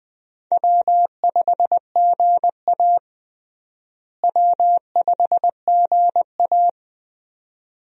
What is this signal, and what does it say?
Morse code, heard as a keyed tone, about 20 wpm, 700 Hz. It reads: W5GA W5GA
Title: USA Call Signs - 4 Characters - Repeated 2x - 20wpm